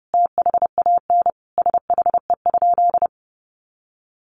Morse code, heard as a keyed tone, about 30 wpm, 700 Hz. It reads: THAN SHE?
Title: Sentences from Top 1200 Words 30wpm